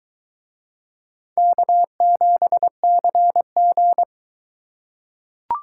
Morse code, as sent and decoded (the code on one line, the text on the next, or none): -.- --... -.-. --.
K7CG